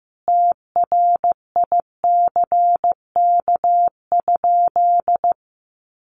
Morse code, heard as a keyed tone, about 15 wpm, 700 Hz. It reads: TRICK?
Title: Sentences from Top 1500 Words 15wpm